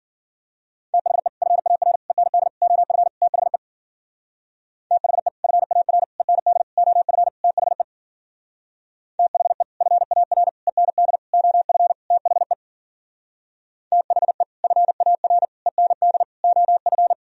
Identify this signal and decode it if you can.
Morse code, heard as a keyed tone, about 51 wpm, 700 Hz. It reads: THE FAR END OF THE THE FAR END OF THE THE FAR END OF THE THE FAR END OF